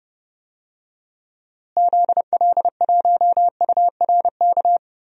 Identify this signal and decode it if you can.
Morse code, heard as a keyed tone, about 30 wpm, 700 Hz. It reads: ZL1URK
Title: DXCC-100 Call Signs - 6 Characters - Repeated 3x - 30wpm